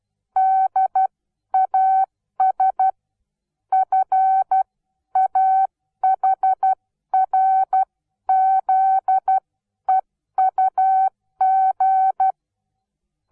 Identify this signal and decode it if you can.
Morse code, heard as a keyed tone, about 12 wpm, spaced as for 10 wpm, 760 Hz. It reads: DAS FAHRZEUG
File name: Onlinelehrgang / morsen / DJ4UF Morsen17